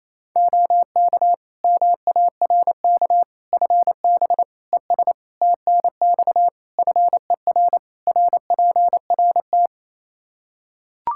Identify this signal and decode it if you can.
Morse code, heard as a keyed tone, about 27 wpm, 700 Hz. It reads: OK MARK FB ES TNX FER RPRT
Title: Practice QSO - Exchange 2 - Info - 28wpm